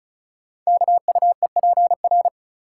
Morse code, heard as a keyed tone, about 36 wpm, 690 Hz. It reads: KUEPR
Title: Code Groups - 5 Letters - 35wpm